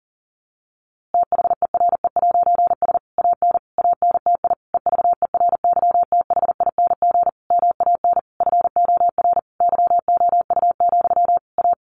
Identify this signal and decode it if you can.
Morse code, heard as a keyed tone, about 40 wpm, 700 Hz. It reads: THERE'S AN ANTI EVERYTHING MAN FOR YOU, A
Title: A Sound Of Thunder 40wpm